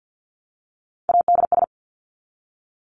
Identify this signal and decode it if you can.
Morse code, heard as a keyed tone, about 50 wpm, 700 Hz. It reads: ADS
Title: Abbreviations Repeated 3x - 50wpm